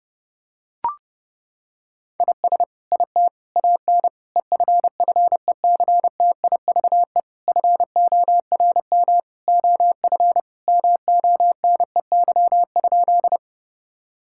Morse code, heard as a keyed tone, about 30 wpm, 700 Hz. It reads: IS IT AN EFFECTIVE FORM OF MONEY?
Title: Sentences from Top 1000 Words - Review Part 6 - 30wpm